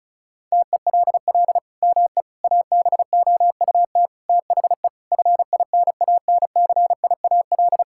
Morse code, heard as a keyed tone, about 36 wpm, 700 Hz. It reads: TELL ME ABOUT THE FINANCIAL
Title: Sentences from Top 1000 Words - Review Part 5 - 35wpm